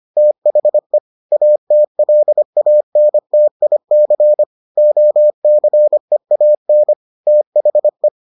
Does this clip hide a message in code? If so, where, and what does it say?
Yes: Morse code from 0.2 to 8.1 s, heard as a keyed tone, about 25 wpm, 600 Hz. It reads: THE ATLANTIC OCEAN THE